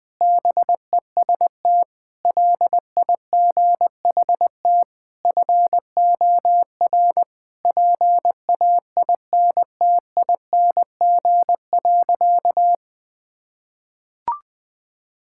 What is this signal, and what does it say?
Morse code, heard as a keyed tone, about 20 wpm, 700 Hz. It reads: BEST LIGHT FOR PAINTING.
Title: Sentences from Top 1000 Words - Review Part 6 - 20wpm